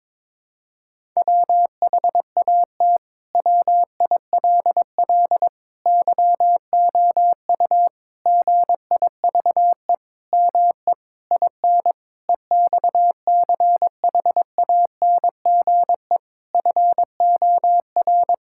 Morse code, heard as a keyed tone, about 22 wpm, 700 Hz. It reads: WHAT WILL YOU GIVE ME IN EXCHANGE FOR
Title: Sentences from Top 1000 Words - Review Part 6 - 22wpm